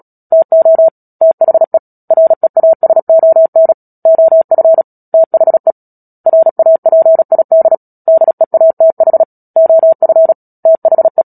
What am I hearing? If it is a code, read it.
Morse code, heard as a keyed tone, about 37 wpm, 650 Hz. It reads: TO THE REASON OF THE RAPID DEATH OF THE
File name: Book Two / 10-Chapter 10 - Epilogue - 36 WPM